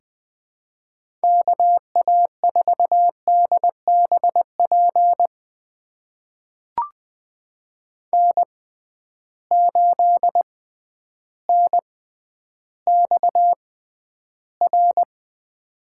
Morse code, heard as a keyed tone, about 20 wpm, spaced as for 11 wpm, 700 Hz. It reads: KA4DBP N8NXR
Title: USA Call Signs - 6x Character Spacing - 20wpm